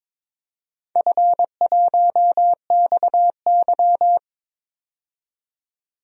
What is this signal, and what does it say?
Morse code, heard as a keyed tone, about 21 wpm, 700 Hz. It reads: F1XY
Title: DXCC-100 Call Signs - 4 Characters - 22wpm